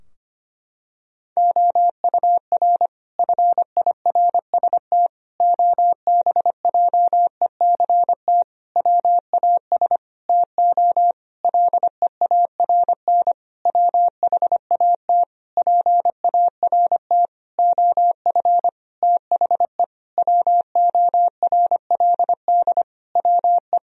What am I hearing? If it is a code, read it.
Morse code, heard as a keyed tone, about 25 wpm, 700 Hz. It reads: OUR FIRST OBJECT WAS TO LEARN WHAT PART OF THE WORLD WE
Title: Sentences from Top 1000 Words - Review Part 3 - 25wpm